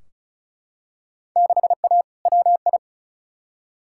Morse code, heard as a keyed tone, about 34 wpm, 700 Hz. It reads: 6A WI
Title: Field Day Exchanges - 35wpm